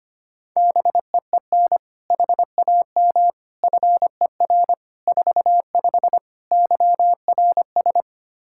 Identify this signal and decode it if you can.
Morse code, heard as a keyed tone, about 25 wpm, 700 Hz. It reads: BEEN HAM FER 45 YRS
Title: Practice QSO - Exchange 3 - More Info - 25wpm